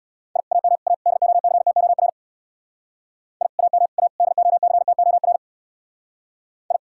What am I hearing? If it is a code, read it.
Morse code, heard as a keyed tone, about 64 wpm, 700 Hz. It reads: I AM A DOCTOR I AM A DOCTOR I